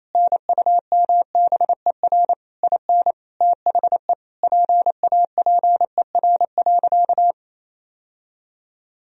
Morse code, heard as a keyed tone, about 28 wpm, 700 Hz. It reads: NUMBER IN THE PAPER.
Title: Sentences from Top 1000 Words - Review Part 1 - 28wpm